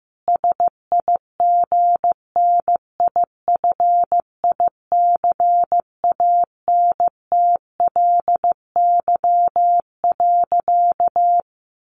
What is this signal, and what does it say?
Morse code, heard as a keyed tone, about 15 wpm, 700 Hz. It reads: SIGNIFICANTLY.
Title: Sentences from Top 1500 Words 15wpm